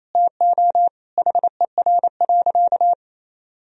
Morse code, heard as a keyed tone, about 27 wpm, 700 Hz. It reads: TO HER.
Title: Sentences from Top 1000 Words - Review Part 3 - 28wpm